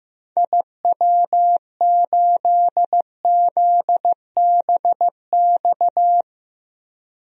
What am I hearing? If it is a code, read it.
Morse code, heard as a keyed tone, about 15 wpm, 700 Hz. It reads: IW8ZBX
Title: DXCC-100 Call Signs - 6 Characters - 15wpm